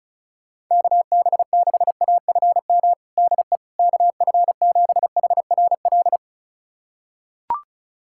Morse code, heard as a keyed tone, about 34 wpm, 700 Hz. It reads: KB6AFM DE KF7HRL
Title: USA Callsign DE Callsign - 6 Characters - 35wpm